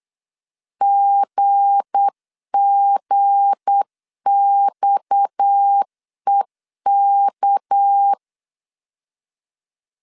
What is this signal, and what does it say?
Morse code, heard as a keyed tone, about 8 wpm, 800 Hz. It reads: GGXEK
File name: LEZIONI / 7 GF9Z / 7 G 8wpm 3118